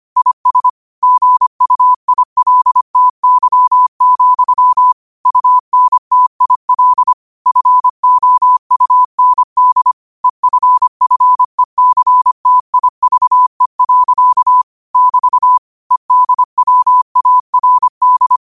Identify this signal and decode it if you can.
Morse code, heard as a keyed tone, about 25 wpm, 1000 Hz. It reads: IS GUILTY, UNTIL FOUND EFFECTIVE. <BT> EDWARD
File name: qotdfiles / 25wpm